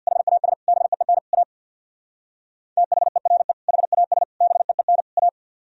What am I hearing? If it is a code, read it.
Morse code, heard as a keyed tone, about 62 wpm, 690 Hz. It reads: HAS BEEN A THERE HAS BEEN A
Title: Speed-Racing - Sets of 4 Words 50wpm